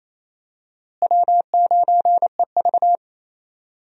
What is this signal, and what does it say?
Morse code, heard as a keyed tone, about 28 wpm, 700 Hz. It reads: W9EV
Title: USA Call Signs - 4 Characters - Repeated 2x - 28wpm